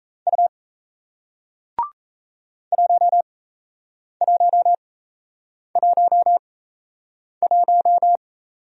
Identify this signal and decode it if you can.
Morse code, heard as a keyed tone, about 37 wpm, spaced as for 8 wpm, 700 Hz. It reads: U 1111